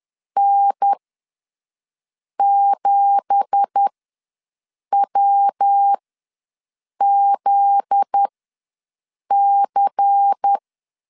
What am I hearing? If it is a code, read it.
Morse code, heard as a keyed tone, about 10 wpm, spaced as for 6 wpm, 800 Hz. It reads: N7WZC
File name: LEZIONI / 10 SIGNS / 10A SLASH ws 10wpm 4121